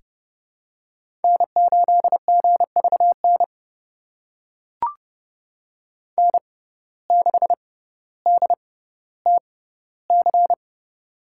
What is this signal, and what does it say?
Morse code, heard as a keyed tone, about 30 wpm, spaced as for 15 wpm, 700 Hz. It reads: N8GVN N6DTC
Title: USA Call Signs - 5 Characters - 6x Character Spacing - 30wpm